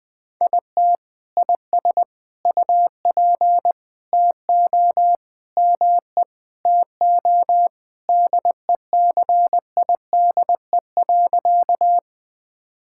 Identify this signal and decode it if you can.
Morse code, heard as a keyed tone, about 20 wpm, 700 Hz. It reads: IT IS UP TO ME TO DECIDE.
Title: Sentences with 2-Word Phrases - 20wpm